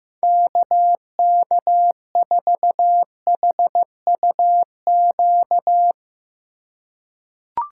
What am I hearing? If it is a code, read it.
Morse code, heard as a keyed tone, about 15 wpm, 700 Hz. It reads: KK4HUQ
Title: USA Call Signs - 6 Characters - 3x Character Spacing - 15wpm